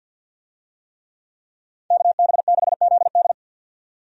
Morse code, heard as a keyed tone, about 49 wpm, 700 Hz. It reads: KB6ZD
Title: USA Call Signs - 5 Characters - 50wpm